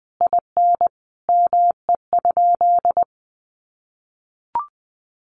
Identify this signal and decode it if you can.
Morse code, heard as a keyed tone, about 20 wpm, 700 Hz. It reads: IN ME?
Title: Sentences from Top 900 Words 20wpm